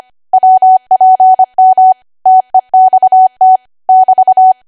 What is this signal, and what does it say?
Morse code, heard as a keyed tone, about 25 wpm, 750 Hz. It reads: WPM TEXT <BT>
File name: cwa / W1AW-Level3 / 25WPM